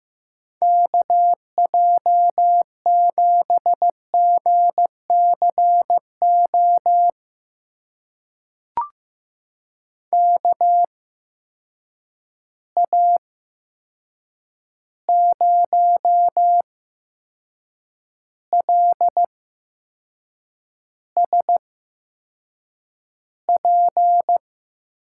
Morse code, heard as a keyed tone, about 15 wpm, spaced as for 6 wpm, 700 Hz. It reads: KJ7GCO KA0LSP
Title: USA Call Signs - 6 Characters - 8x Character Spacing - 15wpm